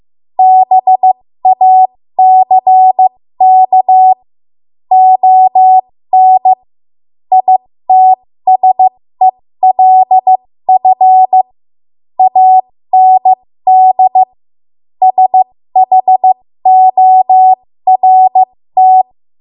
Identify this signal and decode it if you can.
Morse code, heard as a keyed tone, about 15 wpm, spaced as for 13 wpm, 750 Hz. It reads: BACK ON ITSELF AND SHORT
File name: Morse / 13WPM